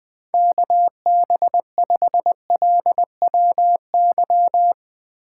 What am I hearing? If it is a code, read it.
Morse code, heard as a keyed tone, about 18 wpm, 700 Hz. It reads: KB5LWY
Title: Speed Racing - USA Call Signs 15wpm